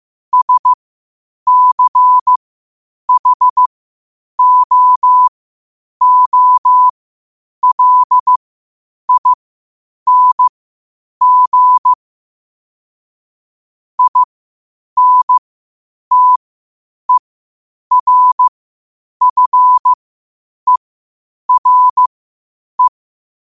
Morse code, heard as a keyed tone, about 15 wpm, spaced as for 8 wpm, 1000 Hz. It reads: SCHOOLING INTERFERE
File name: qotdfiles / 8wpm